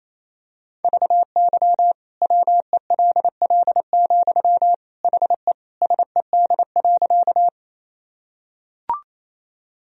Morse code, heard as a keyed tone, about 28 wpm, 700 Hz. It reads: VY WELL, HE SED.
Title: Sentences from Top 200 Words and Abbreviations 28wpm